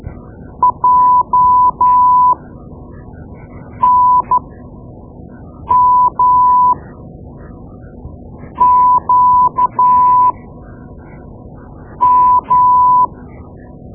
Morse code, heard as a keyed tone, about 13 wpm, spaced as for 5 wpm, 1000 Hz. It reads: JNMQM
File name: Morse Lessons / Morse Lessons / Tape-2B